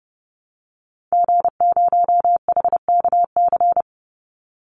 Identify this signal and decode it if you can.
Morse code, heard as a keyed tone, about 30 wpm, 700 Hz. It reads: G0HKC
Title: Call Signs Repeated 2x - 30wpm